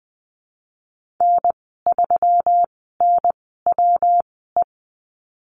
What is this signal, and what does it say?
Morse code, heard as a keyed tone, about 20 wpm, spaced as for 15 wpm, 700 Hz. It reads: N3NWE
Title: 140 Lesson 23g - Letter f - Call Signs 20wpm